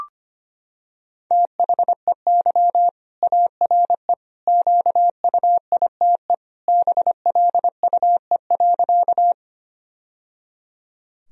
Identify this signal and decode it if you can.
Morse code, heard as a keyed tone, about 25 wpm, 700 Hz. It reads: THEY ARE QUITE BLUE.
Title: Sentences from Top 1000 Words - Review Part 3 - 25wpm